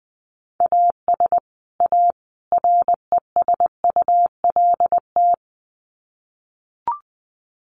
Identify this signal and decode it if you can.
Morse code, heard as a keyed tone, about 20 wpm, 700 Hz. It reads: AS A RESULT
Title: Sets of 3 Words 20wpm